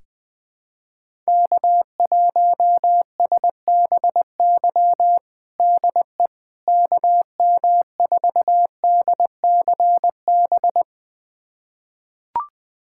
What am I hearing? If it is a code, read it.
Morse code, heard as a keyed tone, about 20 wpm, 700 Hz. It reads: K1SBY DE KM4DCB
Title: USA Callsign DE Callsign 20wpm